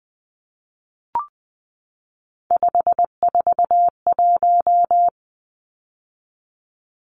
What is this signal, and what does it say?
Morse code, heard as a keyed tone, about 20 wpm, 700 Hz. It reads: 541
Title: Numbers - 3-Digit 20wpm